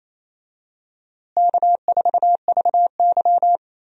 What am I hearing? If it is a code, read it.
Morse code, heard as a keyed tone, about 28 wpm, 700 Hz. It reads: K4VY
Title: USA Call Signs - 4 Characters - Repeated 2x - 28wpm